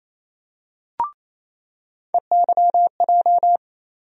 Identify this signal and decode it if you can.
Morse code, heard as a keyed tone, about 27 wpm, 700 Hz. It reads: EYJ
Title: USA Call Sign Postfixes - 3 Letters 28wpm